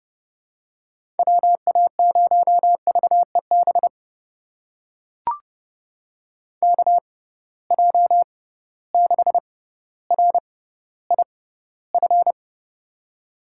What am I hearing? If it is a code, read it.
Morse code, heard as a keyed tone, about 32 wpm, spaced as for 16 wpm, 700 Hz. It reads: WA0VEB KJ6RIF
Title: USA Call Signs - 6 Characters - 6x Character Spacing - 30wpm